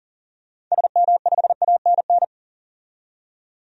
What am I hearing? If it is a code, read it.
Morse code, heard as a keyed tone, about 40 wpm, 700 Hz. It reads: SM5ANN
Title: DXCC-100 Call Signs - 6 Characters - 40wpm